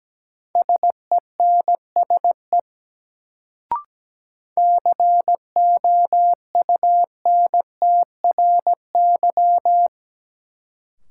Morse code, heard as a keyed tone, about 17 wpm, 700 Hz. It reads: SENSE COUNTRY
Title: Sets of 2 Words - New Word Drill - 17wpm